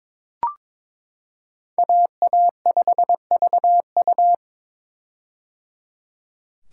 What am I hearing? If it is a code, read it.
Morse code, heard as a keyed tone, about 22 wpm, 700 Hz. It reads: AA5VU
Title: K1USN SST QSO Practice - 22wpm